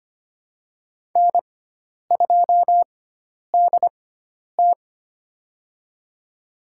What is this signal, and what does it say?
Morse code, heard as a keyed tone, about 25 wpm, spaced as for 10 wpm, 700 Hz. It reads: N2DT
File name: Lesson 16g - Letter c - Call Signs 25wpm